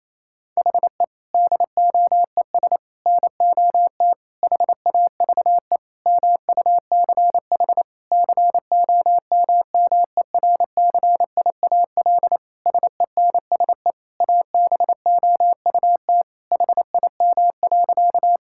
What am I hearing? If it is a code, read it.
Morse code, heard as a keyed tone, about 28 wpm, 700 Hz. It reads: HE DOES NOT HAVE MUCH COMMERCIAL SENSE ABOUT HIM.
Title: Sentences from Top 1000 Words - Review Part 5 - 28wpm